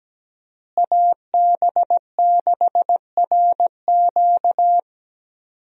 Morse code, heard as a keyed tone, about 17 wpm, 700 Hz. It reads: AB6RQ
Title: USA Call Signs - 5 Characters - Repeated 3x - 17wpm